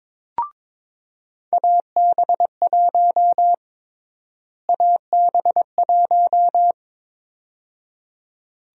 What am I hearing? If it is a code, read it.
Morse code, heard as a keyed tone, about 22 wpm, 700 Hz. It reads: AB1 AB1
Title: USA Call Sign Prefixes - Repeated 2x - 22wpm